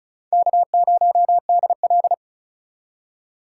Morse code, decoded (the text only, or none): K0DL